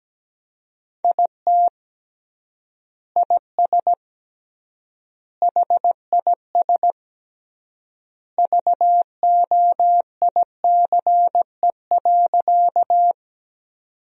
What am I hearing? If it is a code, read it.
Morse code, heard as a keyed tone, about 17 wpm, 700 Hz. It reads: IT IS HIS VOICE.
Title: 3x Word-Spacing - Sentences from Top 500 Words - 17wpm